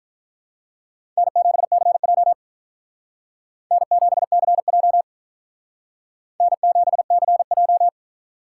Morse code, heard as a keyed tone, about 44 wpm, 700 Hz. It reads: N7CJ N7CJ N7CJ